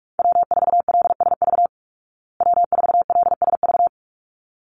Morse code, heard as a keyed tone, about 44 wpm, 700 Hz. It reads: W4LSV W4LSV
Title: Call Signs Repeated 3x - 45wpm